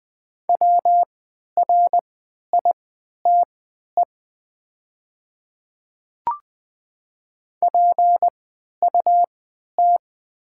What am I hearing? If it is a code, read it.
Morse code, heard as a keyed tone, about 20 wpm, spaced as for 11 wpm, 700 Hz. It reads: WRITE PUT
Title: 3x Character Spacing - Top 200 Words - 20wpm